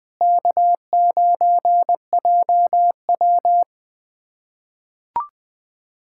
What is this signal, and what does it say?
Morse code, heard as a keyed tone, about 21 wpm, 700 Hz. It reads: K9JW
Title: USA Callsign DE Callsign - 4 Characters - 20wpm